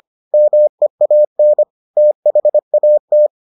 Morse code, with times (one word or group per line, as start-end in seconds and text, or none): 0.3-1.6 MEAN
2.0-3.3 THAT